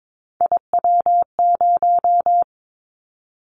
Morse code, heard as a keyed tone, about 22 wpm, 700 Hz. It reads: IW0
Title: Callsign Prefixes for Top 25 DXCC - 22wpm